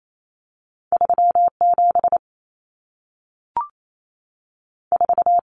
Morse code, heard as a keyed tone, about 28 wpm, 700 Hz. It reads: 37 4
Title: Numbers - 2-Digit 28wpm